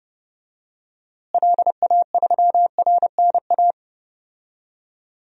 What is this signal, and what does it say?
Morse code, heard as a keyed tone, about 30 wpm, 700 Hz. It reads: LA3RNA
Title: DXCC-100 Call Signs - 6 Characters - Repeated 2x - 30wpm